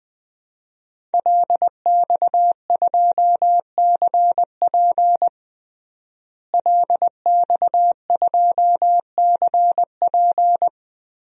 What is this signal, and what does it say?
Morse code, heard as a keyed tone, about 20 wpm, 700 Hz. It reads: LX2CP LX2CP
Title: DXCC-100 Call Signs - 5 Characters - Repeated 3x - 20wpm